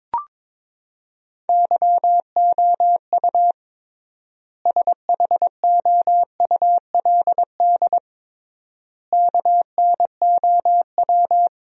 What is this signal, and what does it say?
Morse code, heard as a keyed tone, about 22 wpm, 700 Hz. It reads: YOU SHOULD KNOW
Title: 3x Word-Spacing - Sentences from Top 400 Words - 22wpm